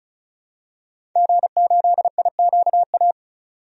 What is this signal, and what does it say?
Morse code, heard as a keyed tone, about 35 wpm, 700 Hz. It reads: G8IQA
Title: DXCC-100 Call Signs - 5 Characters - 35wpm